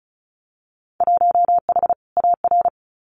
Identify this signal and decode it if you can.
Morse code, heard as a keyed tone, about 36 wpm, 700 Hz. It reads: 1H AR
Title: Winter Field Day Exchanges 35wpm